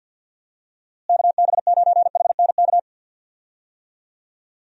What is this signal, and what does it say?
Morse code, heard as a keyed tone, about 48 wpm, 700 Hz. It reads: KB9HNK
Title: USA Call Signs - 6 Characters - Repeated 3x - 50wpm